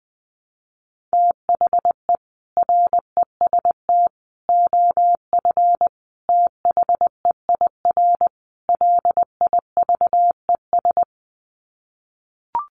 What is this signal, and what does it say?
Morse code, heard as a keyed tone, about 20 wpm, 700 Hz. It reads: THE REST OF THEIR LIVES
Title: Sets of 5 Words Encore 20wpm